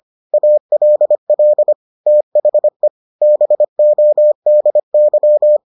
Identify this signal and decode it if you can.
Morse code, heard as a keyed tone, about 25 wpm, 600 Hz. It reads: ALL THE BODY